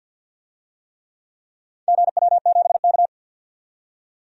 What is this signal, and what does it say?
Morse code, heard as a keyed tone, about 50 wpm, 700 Hz. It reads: GW7K
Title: DXCC-100 Call Signs - 4 Characters - 50wpm